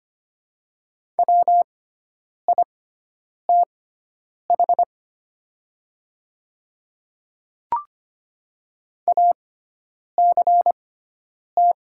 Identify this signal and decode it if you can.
Morse code, heard as a keyed tone, about 25 wpm, spaced as for 9 wpm, 700 Hz. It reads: WITH ACT
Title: 6x Character Spacing - Top 200 Words - 25wpm